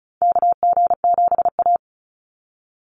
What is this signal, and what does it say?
Morse code, heard as a keyed tone, about 36 wpm, 700 Hz. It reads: KG7A
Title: Call Signs - 35wpm